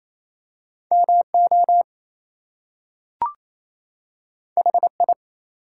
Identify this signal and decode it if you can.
Morse code, heard as a keyed tone, about 28 wpm, 700 Hz. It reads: MO HI
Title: US States and Canadian Provinces 28wpm